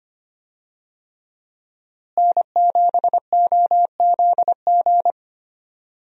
Morse code, heard as a keyed tone, about 25 wpm, 700 Hz. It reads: N7OZG
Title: USA Call Signs - 5 Characters - Repeated 3x - 25wpm